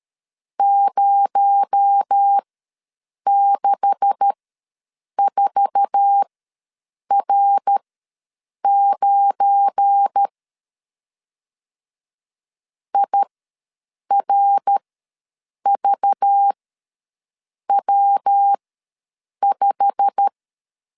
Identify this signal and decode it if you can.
Morse code, heard as a keyed tone, about 13 wpm, spaced as for 7 wpm, 800 Hz. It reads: 064R9 IRVW5
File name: LEZIONI / 9 286 ws 12wpm 3351